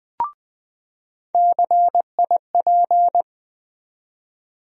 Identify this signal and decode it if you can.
Morse code, heard as a keyed tone, about 20 wpm, 700 Hz. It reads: CIP
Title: USA Call Sign Postfixes - 3 Letters 20wpm